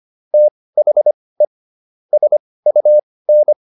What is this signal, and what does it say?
Morse code, heard as a keyed tone, about 25 wpm, spaced as for 18 wpm, 600 Hz. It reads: THE SUN